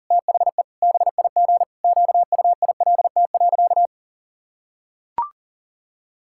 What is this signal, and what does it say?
Morse code, heard as a keyed tone, about 37 wpm, 700 Hz. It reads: THE BIG QUILT.
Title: Pangram Challenge v2 - 40wpm